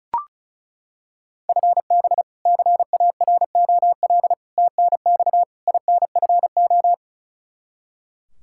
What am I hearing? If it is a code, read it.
Morse code, heard as a keyed tone, about 34 wpm, 700 Hz. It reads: FB CAROL TNX INFO
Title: Practice QSO - Exchange 3 - More Info - 35wpm